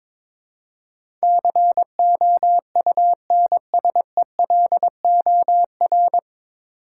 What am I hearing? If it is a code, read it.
Morse code, heard as a keyed tone, about 22 wpm, 700 Hz. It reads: COUNSELOR
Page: Morse Code Ninja practice 22wpm